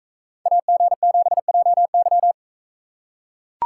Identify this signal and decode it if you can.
Morse code, heard as a keyed tone, about 43 wpm, 700 Hz. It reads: AG7JY